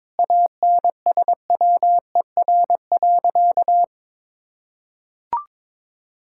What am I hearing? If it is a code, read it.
Morse code, heard as a keyed tone, about 22 wpm, 700 Hz. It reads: ANSWER.
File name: practice / 22wpm / Sentences with 4-Word Phrases - 22wpm